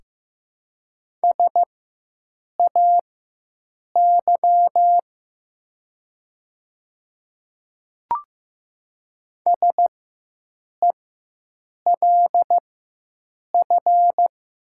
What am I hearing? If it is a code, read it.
Morse code, heard as a keyed tone, about 15 wpm, spaced as for 7 wpm, 700 Hz. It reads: SAY SELF
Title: 4x Character Spacing - Top 500 Words - 15wpm